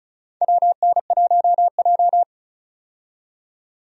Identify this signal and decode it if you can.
Morse code, heard as a keyed tone, about 34 wpm, 700 Hz. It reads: WN1J
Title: USA Call Signs - 4 Characters - 35wpm